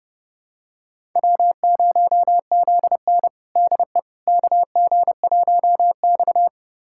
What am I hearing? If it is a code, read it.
Morse code, heard as a keyed tone, about 32 wpm, 700 Hz. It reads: W0ZN DE KG1X
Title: USA Callsign DE Callsign - 4 Characters - 30wpm